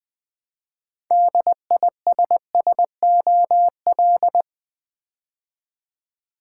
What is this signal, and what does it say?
Morse code, heard as a keyed tone, about 20 wpm, 700 Hz. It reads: DISSOL
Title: Build A Word - 3 Letters At A Time - 8-Letter Words - 20wpm